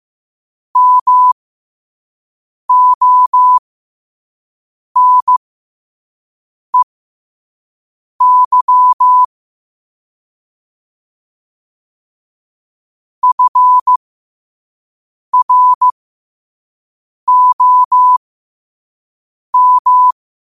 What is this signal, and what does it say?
Morse code, heard as a keyed tone, about 15 wpm, spaced as for 5 wpm, 1000 Hz. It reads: MONEY FROM
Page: New quotes every day in morse code at 5 Words per minute.